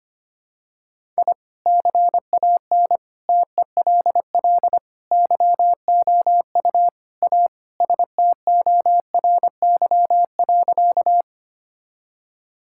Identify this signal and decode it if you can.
Morse code, heard as a keyed tone, about 25 wpm, 700 Hz. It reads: I CAN TELL YOU A STORY.
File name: practice / 25wpm / Sentences with 4-Word Phrases - 25wpm